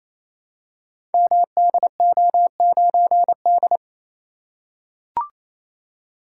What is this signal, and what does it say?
Morse code, heard as a keyed tone, about 27 wpm, 700 Hz. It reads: MDO9D
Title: Code Groups - 5 Letter:Numbers - 28wpm